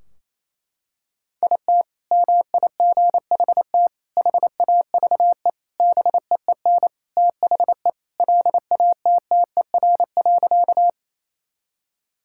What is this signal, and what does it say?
Morse code, heard as a keyed tone, about 28 wpm, 700 Hz. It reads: IT MIGHT HAVE BEEN THE LATTER.
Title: Sentences from Top 1600 Words 28wpm